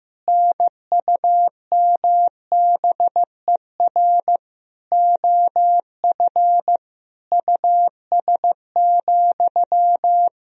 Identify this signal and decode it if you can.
Morse code, heard as a keyed tone, about 16 wpm, 700 Hz. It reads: NUMBER OF US,
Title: Sentences from Top 1000 Words - Review Part 1 - 15wpm